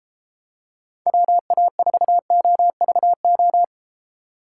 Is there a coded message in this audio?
Yes: Morse code from 1.1 to 3.6 s, heard as a keyed tone, about 33 wpm, 700 Hz. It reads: WA4OVO